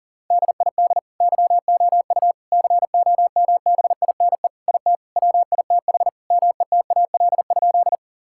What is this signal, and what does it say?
Morse code, heard as a keyed tone, about 40 wpm, 700 Hz. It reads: DID YOU COMBINE IT WITH METAL?
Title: Sentences from Top 1600 Words 40wpm